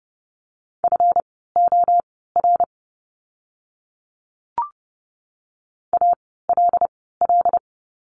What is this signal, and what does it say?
Morse code, heard as a keyed tone, about 30 wpm, spaced as for 17 wpm, 700 Hz. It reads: FOR ALL
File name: course / 30:10wpm / 237 Lesson 38f - Letter q - Words 30wpm